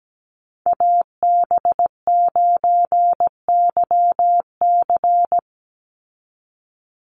Morse code, heard as a keyed tone, about 17 wpm, 700 Hz. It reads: AB9YC
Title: Call Signs Repeated 2x - 17wpm